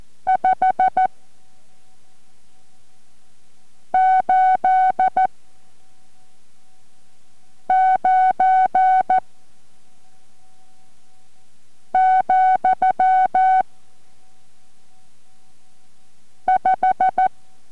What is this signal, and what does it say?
Morse code, heard as a keyed tone, about 14 wpm, spaced as for 3 wpm, 750 Hz. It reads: 589, 5